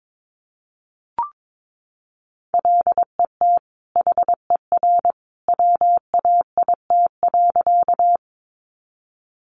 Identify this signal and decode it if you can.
Morse code, heard as a keyed tone, about 22 wpm, 700 Hz. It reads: LET HER WAIT.
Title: Sentences from Top 500 Words 22wpm